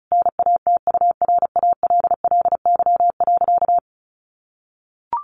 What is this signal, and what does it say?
Morse code, heard as a keyed tone, about 34 wpm, 700 Hz. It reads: NATURALLY.
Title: Sentences from Top 1300 Words 35wpm